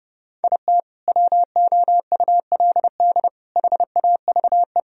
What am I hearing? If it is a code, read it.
Morse code, heard as a keyed tone, about 30 wpm, 700 Hz. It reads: IT WOULD HAVE
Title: Sentences from Top 1000 Words - Review Part 1 - 30wpm